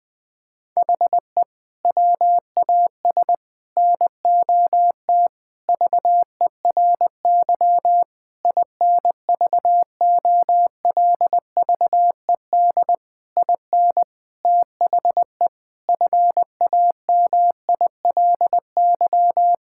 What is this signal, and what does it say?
Morse code, heard as a keyed tone, about 21 wpm, 700 Hz. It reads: HE WAS NOT VERY INVOLVED IN THE FAMILY
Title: Sentences from Top 1000 Words - Review Part 5 - 20wpm